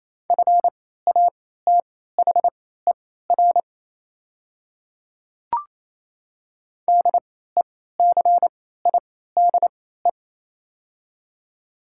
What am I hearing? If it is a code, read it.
Morse code, heard as a keyed tone, about 28 wpm, spaced as for 16 wpm, 700 Hz. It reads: FATHER DECIDE
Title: 3x Character Spacing - Top 500 Words - 28wpm